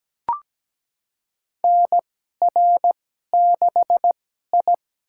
Morse code, heard as a keyed tone, about 17 wpm, spaced as for 12 wpm, 700 Hz. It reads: NR6I
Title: USA Call Signs - 4 Characters - 2x Character Spacing - 17wpm